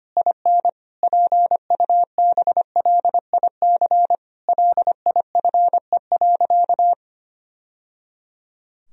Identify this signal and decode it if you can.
Morse code, heard as a keyed tone, about 25 wpm, 700 Hz. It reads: IN PUBLIC LIFE.
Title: Sentences from Top 1000 Words - Review Part 3 - 25wpm